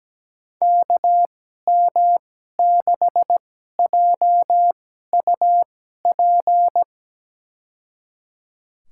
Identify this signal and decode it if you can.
Morse code, heard as a keyed tone, about 17 wpm, spaced as for 12 wpm, 700 Hz. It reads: KM6JUP
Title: USA Call Signs - 6 Characters - 2x Character Spacing - 17wpm